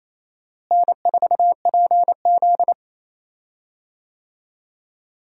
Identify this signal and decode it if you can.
Morse code, heard as a keyed tone, about 28 wpm, 700 Hz. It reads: N4PZ
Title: USA Call Signs - 4 Characters - Repeated 2x - 28wpm